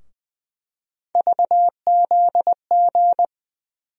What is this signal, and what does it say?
Morse code, heard as a keyed tone, about 20 wpm, 700 Hz. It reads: VZG
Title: USA Call Sign Postfixes - 3 Letters 20wpm